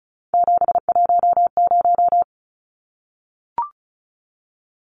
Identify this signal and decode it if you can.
Morse code, heard as a keyed tone, about 34 wpm, 700 Hz. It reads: 710
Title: Numbers - 3-Digit 35wpm